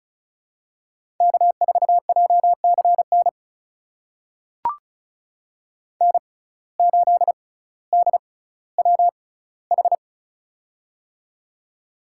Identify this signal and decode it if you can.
Morse code, heard as a keyed tone, about 34 wpm, spaced as for 18 wpm, 700 Hz. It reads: K4JCN N8DWH
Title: USA Call Signs - 5 Characters - 6x Character Spacing - 35wpm